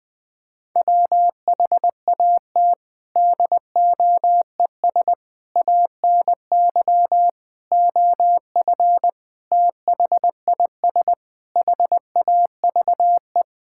Morse code, heard as a keyed tone, about 20 wpm, 700 Hz. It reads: WHAT DOES ANY OF THIS HAVE
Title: Sentences with 9-Word Phrases - Spoken Beginning - 20wpm